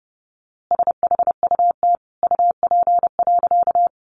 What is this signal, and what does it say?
Morse code, heard as a keyed tone, about 30 wpm, 700 Hz. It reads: SHUT UP.
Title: A Sound Of Thunder 30wpm